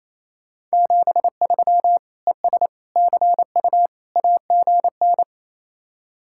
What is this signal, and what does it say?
Morse code, heard as a keyed tone, about 34 wpm, 700 Hz. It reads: 73 ES CU AGN
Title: Speed-Racing Common QSO Phrases - 28wpm